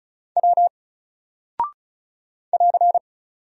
Morse code, heard as a keyed tone, about 36 wpm, 700 Hz. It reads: W <AR>
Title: Single Character (Letter:Number:Punctuation:Prosign) 35wpm